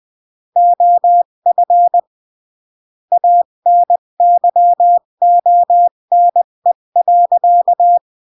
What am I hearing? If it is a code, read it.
Morse code, heard as a keyed tone, about 20 wpm, 700 Hz. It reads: OF ANYONE.
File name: CW / jodorowsky / 82 commandments 20-15